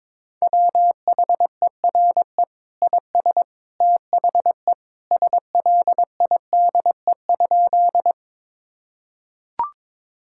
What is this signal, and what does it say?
Morse code, heard as a keyed tone, about 22 wpm, 700 Hz. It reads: WHERE IS THE SLIDE?
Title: Sentences from Top 1600 Words 22wpm